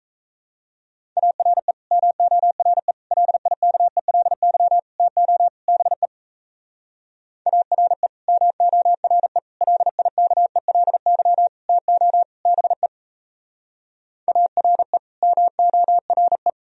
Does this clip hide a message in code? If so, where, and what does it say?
Yes: Morse code from 1.2 to 16.5 s, heard as a keyed tone, about 37 wpm, 700 Hz. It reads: ARE MORE LIKELY TO BE ARE MORE LIKELY TO BE ARE MORE